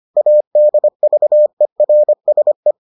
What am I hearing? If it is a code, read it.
Morse code, heard as a keyed tone, about 25 wpm, 600 Hz. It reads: ADVERSE